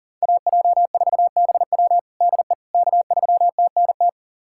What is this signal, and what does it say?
Morse code, heard as a keyed tone, about 40 wpm, 700 Hz. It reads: AJ4BW DE K3TNT
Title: USA Callsign DE Callsign - 5 Characters - 40wpm